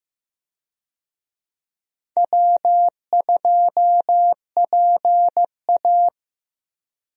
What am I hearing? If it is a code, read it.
Morse code, heard as a keyed tone, about 15 wpm, 700 Hz. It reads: W2PA
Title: USA Call Signs - 4 Characters - 15wpm